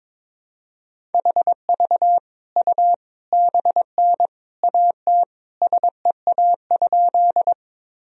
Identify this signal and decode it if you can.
Morse code, heard as a keyed tone, about 22 wpm, 700 Hz. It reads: HV U BN AT SEA?
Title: Sentences from Top 300 Words and Abbreviations 22wpm